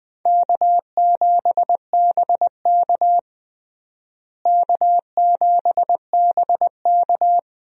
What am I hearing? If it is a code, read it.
Morse code, heard as a keyed tone, about 21 wpm, 700 Hz. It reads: K7BK K7BK
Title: USA Call Signs - 4 Characters - Repeated 3x - 20wpm